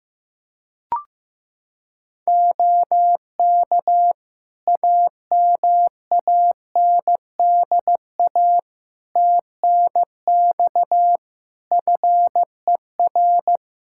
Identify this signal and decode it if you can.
Morse code, heard as a keyed tone, about 15 wpm, 700 Hz. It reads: OK AMANDA TNX FER
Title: K1USN SST QSO Practice 15wpm